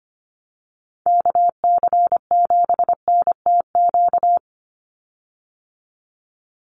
Morse code, heard as a keyed tone, about 25 wpm, 700 Hz. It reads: KC7NTQ
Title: USA Call Signs 25wpm